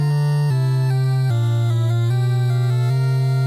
There is no code.